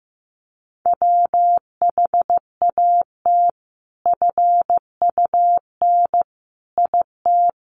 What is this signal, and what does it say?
Morse code, heard as a keyed tone, about 15 wpm, 700 Hz. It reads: WHAT FUN IT
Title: Sentences from Top 700 Words 15wpm